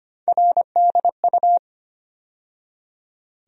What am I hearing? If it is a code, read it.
Morse code, heard as a keyed tone, about 25 wpm, 700 Hz. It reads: RDU
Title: USA Call Sign Postfixes - 3 Letters 25wpm